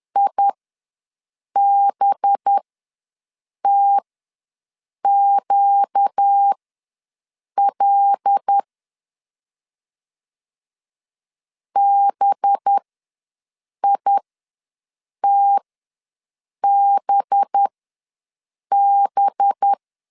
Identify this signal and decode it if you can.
Morse code, heard as a keyed tone, about 10 wpm, spaced as for 6 wpm, 800 Hz. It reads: IBTQL BITBB
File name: LEZIONI / 5 LBVC / 5 LB ws 10wpm 3616